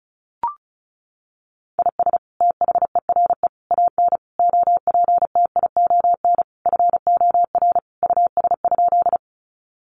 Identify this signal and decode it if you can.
Morse code, heard as a keyed tone, about 34 wpm, 700 Hz. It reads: IS THERE AN OPTION FOR US?
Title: Sentences from Top 1300 Words 35wpm